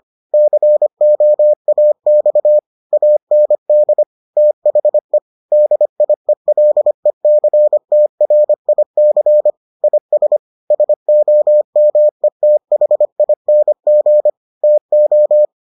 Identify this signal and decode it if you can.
Morse code, heard as a keyed tone, about 25 wpm, 600 Hz. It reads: COAX AND THE DIELECTRIC IS SOMETHING TO